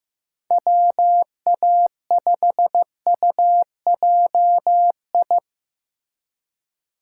Morse code, heard as a keyed tone, about 15 wpm, 700 Hz. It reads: WA5UJI
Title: USA Call Signs - 6 Characters - 3x Character Spacing - 15wpm